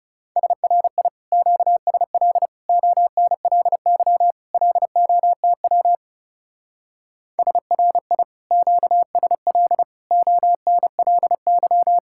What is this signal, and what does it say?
Morse code, heard as a keyed tone, about 36 wpm, 700 Hz. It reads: SRI QSL ONLY LOTW SRI QSL ONLY
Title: Speed-Racing Common QSO Phrases Encore - 30wpm